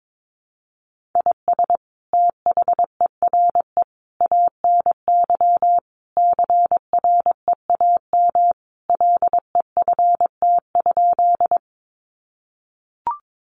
Morse code, heard as a keyed tone, about 22 wpm, 700 Hz. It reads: IS THERE ANY CREAM LEFT?
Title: Sentences from Top 1500 Words 22wpm